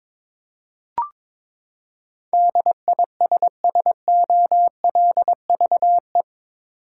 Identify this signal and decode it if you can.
Morse code, heard as a keyed tone, about 22 wpm, 700 Hz. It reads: DISSOLVE
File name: practice / 22wpm / Eight Letter Words - 22wpm